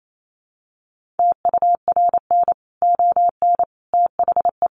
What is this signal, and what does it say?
Morse code, heard as a keyed tone, about 28 wpm, 700 Hz. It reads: TURN ON THE
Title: Sentences from Top 1400 Words 28wpm